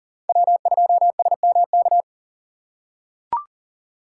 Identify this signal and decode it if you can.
Morse code, heard as a keyed tone, about 40 wpm, 690 Hz. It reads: W2SMK
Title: USA Call Signs - 4x Character Spacing - 40wpm